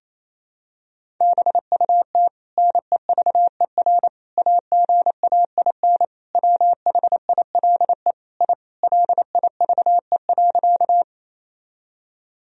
Morse code, heard as a keyed tone, about 27 wpm, 700 Hz. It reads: BUT NEVER AGAIN WHILE I LIVE.
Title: Sentences from Top 1000 Words - Review Part 1 - 28wpm